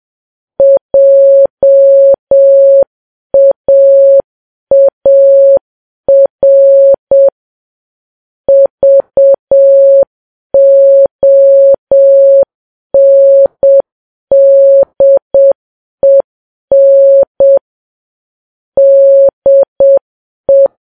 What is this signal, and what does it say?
Morse code, heard as a keyed tone, about 7 wpm, 550 Hz. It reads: JAAR VONDEN DE
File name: Files Web tls / Morse on9fv / morse7kps